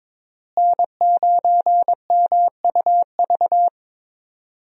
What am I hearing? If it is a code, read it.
Morse code, heard as a keyed tone, about 22 wpm, 700 Hz. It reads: N9MUV
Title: USA Call Signs - 5 Characters - Repeated 2x - 22wpm